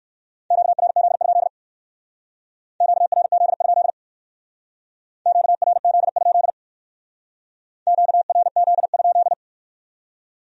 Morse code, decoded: QRZ? QRZ? QRZ? QRZ?

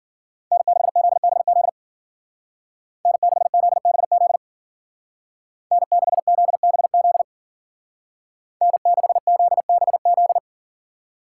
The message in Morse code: N6ZBZ N6ZBZ N6ZBZ N6ZBZ

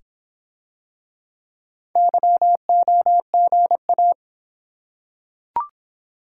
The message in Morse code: YOGA

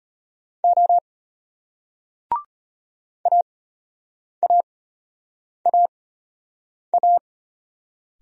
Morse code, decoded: O AAAA

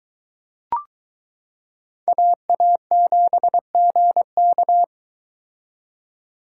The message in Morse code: AA7GK

